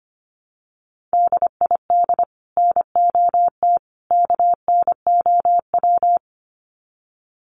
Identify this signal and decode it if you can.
Morse code, heard as a keyed tone, about 25 wpm, 700 Hz. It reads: DID NOT KNOW
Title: Sets of 3 Words 25wpm